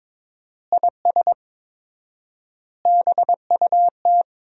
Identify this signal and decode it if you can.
Morse code, heard as a keyed tone, about 22 wpm, 700 Hz. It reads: IS BUT
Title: 4x Word-Spacing - Sentences from Top 100 Words - 22wpm